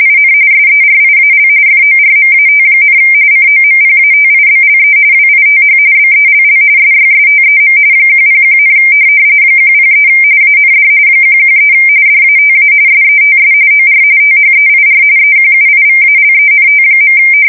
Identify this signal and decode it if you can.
RTTY, about 45.45 baud. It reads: IN SERIAL FORMAT, COMPATIBLE WITH BELL SYSTEM 101, 103 AND 113 DATA SETS OR THEIR EQUIVALENT.
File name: fsk170 / teletype-corp